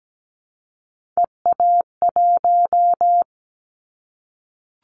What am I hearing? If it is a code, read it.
Morse code, heard as a keyed tone, about 17 wpm, 700 Hz. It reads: EA1
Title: Callsign Prefixes for Top 25 DXCC - 17wpm